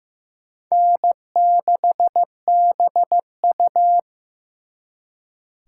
Morse code, heard as a keyed tone, about 18 wpm, 700 Hz. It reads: N6BU
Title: Speed Racing - USA 1x2 Call Signs 15wpm